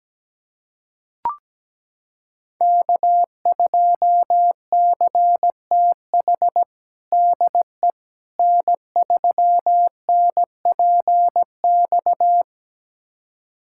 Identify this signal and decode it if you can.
Morse code, heard as a keyed tone, about 17 wpm, 700 Hz. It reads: K2CTH DE N3NPX
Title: USA Callsign DE Callsign - 5 Characters - 17wpm